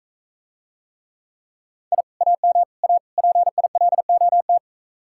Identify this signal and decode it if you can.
Morse code, heard as a keyed tone, about 45 wpm, 700 Hz. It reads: I AM A PILOT